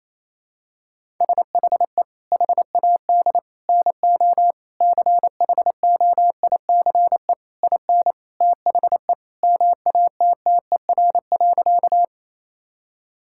Morse code, heard as a keyed tone, about 28 wpm, 700 Hz. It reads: SHE HAD NO CHOICE IN THE MATTER.
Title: Sentences from Top 1000 Words - Review Part 6 - 28wpm